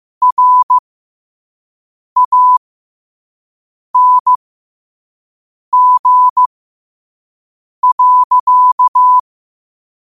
Morse code, heard as a keyed tone, about 15 wpm, spaced as for 5 wpm, 1000 Hz. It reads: RANG.
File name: qotdfiles / 5wpm